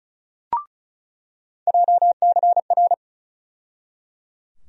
Morse code, heard as a keyed tone, about 34 wpm, 700 Hz. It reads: JCR
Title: USA Call Sign Postfixes - 3 Letters 35wpm